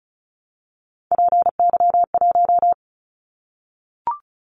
Morse code, heard as a keyed tone, about 36 wpm, 690 Hz. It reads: PY1
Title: Callsign Prefixes for Top 50 DXCC - 35wpm